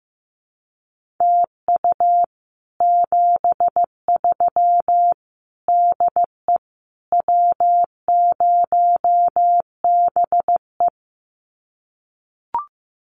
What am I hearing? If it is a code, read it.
Morse code, heard as a keyed tone, about 15 wpm, 700 Hz. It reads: TU 73 DE W0BE
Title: Common QSO Phrases Encore 15wpm